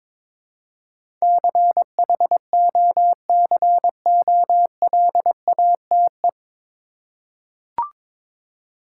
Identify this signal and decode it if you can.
Morse code, heard as a keyed tone, about 22 wpm, 700 Hz. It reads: CHOCOLATE
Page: Morse Code Ninja practice 22wpm